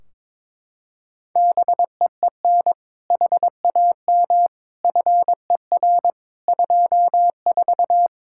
Morse code, heard as a keyed tone, about 22 wpm, 700 Hz. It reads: BEEN HAM FER 24
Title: Practice QSO - Exchange 3 - More Info - 22wpm